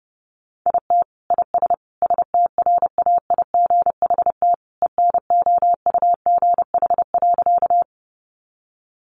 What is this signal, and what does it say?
Morse code, heard as a keyed tone, about 30 wpm, 700 Hz. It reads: IT IS STRAIGHT ENOUGH.
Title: Sentences from Top 1000 Words 30wpm